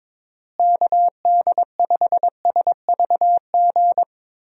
Morse code, heard as a keyed tone, about 22 wpm, 700 Hz. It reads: KD5SVG
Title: USA Call Signs - 6 Characters - Repeated 2x - 22wpm